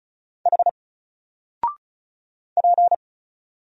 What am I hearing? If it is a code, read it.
Morse code, heard as a keyed tone, about 35 wpm, 700 Hz. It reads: H P
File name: Single-Letter-Number-35wpm